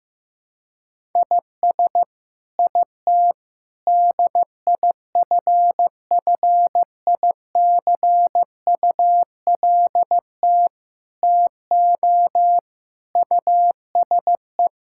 Morse code, heard as a keyed tone, about 15 wpm, 700 Hz. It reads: IS IT DIFFICULT TO USE
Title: Sentences from Top 1000 Words - Review Part 6 - 15wpm